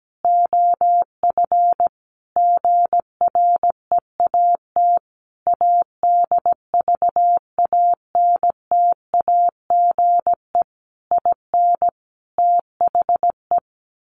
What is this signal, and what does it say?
Morse code, heard as a keyed tone, about 17 wpm, 700 Hz. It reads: OF GREAT ADVANTAGE IN THE
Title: Sentences from Top 900 Words 17wpm